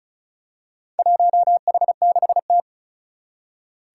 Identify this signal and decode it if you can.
Morse code, heard as a keyed tone, about 36 wpm, 700 Hz. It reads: 1H6T
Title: Four Challenging Letter-Number Combinations 35wpm